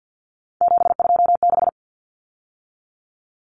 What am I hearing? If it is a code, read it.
Morse code, heard as a keyed tone, about 50 wpm, 700 Hz. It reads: 726
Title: Numbers - 3-Digit 50wpm